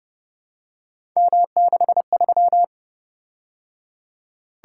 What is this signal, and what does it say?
Morse code, heard as a keyed tone, about 29 wpm, 700 Hz. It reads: M63
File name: Speed-Racing - Three Letter-Number Combinations - 25wpm